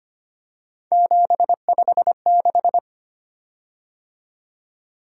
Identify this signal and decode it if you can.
Morse code, heard as a keyed tone, about 25 wpm, 700 Hz. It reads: 756
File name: Three Letter-Number Combinations - 25wpm